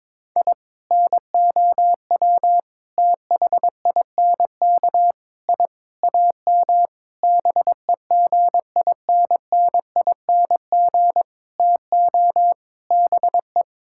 Morse code, heard as a keyed tone, about 22 wpm, 700 Hz. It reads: I NOW THINK I AM BEGINNING TO BE
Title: Sentences from Top 1000 Words - Review Part 5 - 22wpm